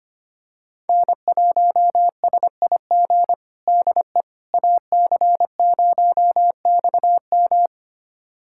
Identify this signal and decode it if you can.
Morse code, heard as a keyed tone, about 25 wpm, 700 Hz. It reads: N1SIG DE AC0XM
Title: USA Callsign DE Callsign - 5 Characters - 25wpm